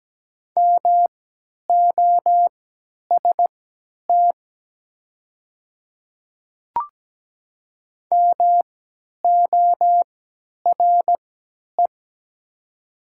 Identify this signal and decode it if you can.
Morse code, heard as a keyed tone, about 17 wpm, spaced as for 10 wpm, 700 Hz. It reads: MOST MORE
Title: 3x Character Spacing - Top 100 Words - 17wpm